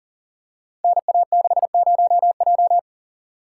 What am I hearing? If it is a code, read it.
Morse code, heard as a keyed tone, about 38 wpm, 700 Hz. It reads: NA60J
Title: Code Groups - 5 Letter:Numbers - 40wpm